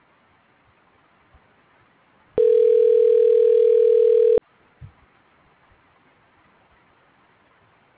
Ringback tone (North America), 440 + 480 Hz.